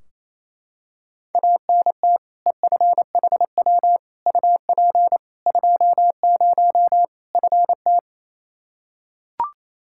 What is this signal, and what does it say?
Morse code, heard as a keyed tone, about 28 wpm, 700 Hz. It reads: ANT EFHW UP 20 FT
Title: Practice QSO - Exchange 2 - Info - 28wpm